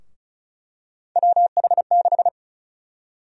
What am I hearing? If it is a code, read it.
Morse code, heard as a keyed tone, about 36 wpm, 700 Hz. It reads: WH6
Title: USA Call Sign Prefixes - Repeated 2x - 35wpm